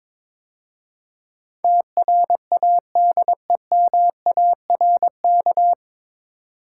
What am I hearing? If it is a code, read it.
Morse code, heard as a keyed tone, about 23 wpm, 700 Hz. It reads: TRADEMARK
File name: 9 Letter Words - First 3 Letters Given - 22wpm